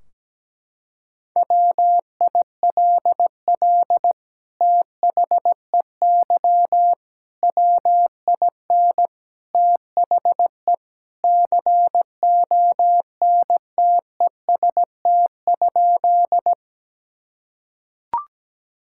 Morse code, heard as a keyed tone, about 17 wpm, 700 Hz. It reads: WILL THEY WIN THE CONTEST?
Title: Sentences from Top 1600 Words 17wpm